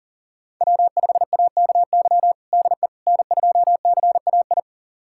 Morse code, heard as a keyed tone, about 37 wpm, 700 Hz. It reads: W5AKY DE N2CAI
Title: USA Callsign DE Callsign - 5 Characters - 40wpm